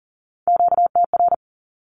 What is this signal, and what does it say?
Morse code, heard as a keyed tone, about 42 wpm, 700 Hz. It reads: QTR
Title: Q-Codes with Spoken Meaning - 40wpm